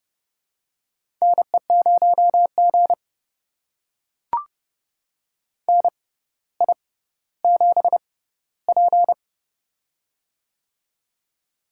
Morse code, heard as a keyed tone, about 32 wpm, spaced as for 16 wpm, 700 Hz. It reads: NE0G NI7P
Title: USA Call Signs - 4 Characters - 6x Character Spacing - 30wpm